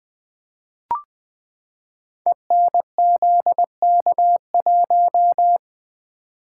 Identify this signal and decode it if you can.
Morse code, heard as a keyed tone, about 20 wpm, 700 Hz. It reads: ENZK1
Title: Code Groups - 5 Letter:Numbers - 20wpm